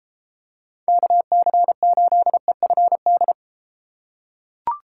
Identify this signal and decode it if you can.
Morse code, heard as a keyed tone, about 33 wpm, 700 Hz. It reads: KC8EFD